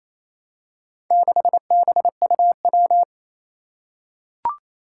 Morse code, heard as a keyed tone, about 27 wpm, 700 Hz. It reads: 6BUW
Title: Four Challenging Letter-Number Combinations 28wpm